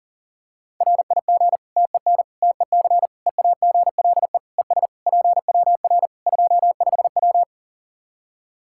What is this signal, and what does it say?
Morse code, heard as a keyed tone, about 37 wpm, 700 Hz. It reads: RIG TEN TEC EAGLE ES PWR 25W
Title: K1USN SST QSO Practice 40wpm